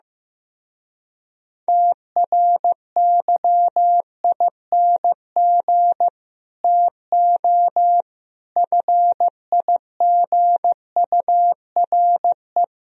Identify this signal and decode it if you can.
Morse code, heard as a keyed tone, about 15 wpm, 700 Hz. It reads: TRYING TO FIGURE
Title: Speed-Racing - Sets of 5 Words 15wpm